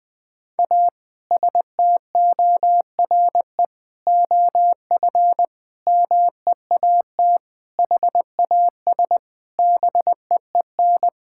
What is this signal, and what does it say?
Morse code, heard as a keyed tone, about 20 wpm, 700 Hz. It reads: A STORE OF MEAT HAS BEEN
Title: Sentences from Top 1000 Words - Review Part 5 - 20wpm